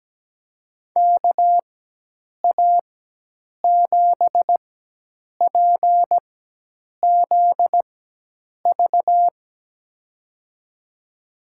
Morse code, heard as a keyed tone, about 17 wpm, spaced as for 8 wpm, 700 Hz. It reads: KA7PZV
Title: USA Call Signs - 6 Characters - 4x Character Spacing - 17wpm